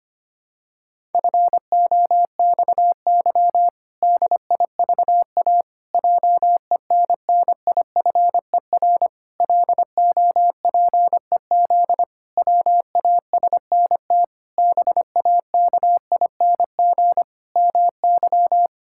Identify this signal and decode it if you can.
Morse code, heard as a keyed tone, about 25 wpm, 700 Hz. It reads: FOXY DIVA JENNIFER LOPEZ WASNT BAKING MY
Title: Pangram Challenge v2 - 25wpm